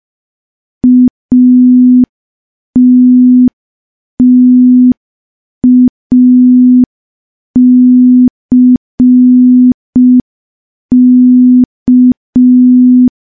Morse code, heard as a keyed tone, about 5 wpm, 250 Hz. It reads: ATTACK
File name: morse / morse-training-05WPM